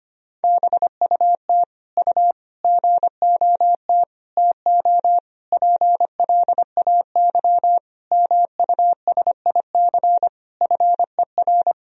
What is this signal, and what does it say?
Morse code, heard as a keyed tone, about 25 wpm, 700 Hz. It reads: BUT U GOT TO PLAY MUSIC FER
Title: Sentences from Top 300 Words and Abbreviations 25wpm